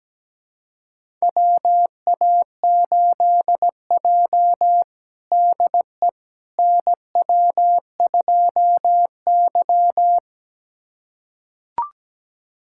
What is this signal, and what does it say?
Morse code, heard as a keyed tone, about 17 wpm, 700 Hz. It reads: WA8J DE NW2Y
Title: USA Callsign DE Callsign - 4 Characters - 17wpm